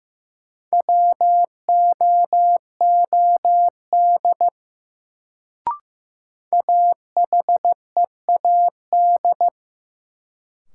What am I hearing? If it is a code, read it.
Morse code, heard as a keyed tone, about 15 wpm, 700 Hz. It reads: WOOD AHEAD
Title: Sentences from Top 1100 Words - New Word Drill Alt - 15wpm